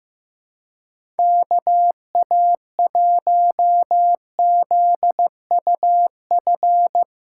Morse code, heard as a keyed tone, about 15 wpm, 700 Hz. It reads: KA1ZUF
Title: USA Call Signs - 6 Characters - 3x Character Spacing - 15wpm